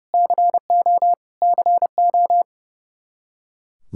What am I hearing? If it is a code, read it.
Morse code, heard as a keyed tone, about 30 wpm, 700 Hz. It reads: CO CO
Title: Practice QSO - Exchange 1 - Beginning - 30wpm